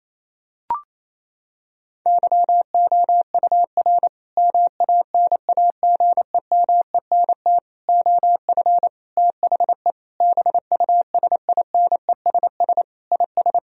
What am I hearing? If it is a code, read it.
Morse code, heard as a keyed tone, about 27 wpm, 700 Hz. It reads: YOUR MANAGEMENT OF THE BUSINESS IS
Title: Sentences from Top 1000 Words - Review Part 4 - 28wpm